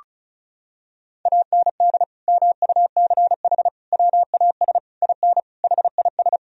Morse code, heard as a keyed tone, about 36 wpm, 700 Hz. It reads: AND MUCH WAS IN HIS
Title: Sentences from Top 1000 Words - Review Part 2 - 35wpm